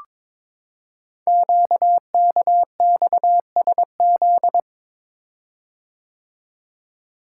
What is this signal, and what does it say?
Morse code, heard as a keyed tone, about 22 wpm, 700 Hz. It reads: QKXSZ
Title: Code Groups - 5 Letters - 22wpm